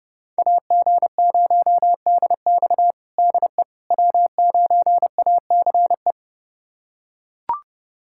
Morse code, heard as a keyed tone, about 30 wpm, 700 Hz. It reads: AG0DX DE W9ACE
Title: USA Callsign DE Callsign - 5 Characters - 30wpm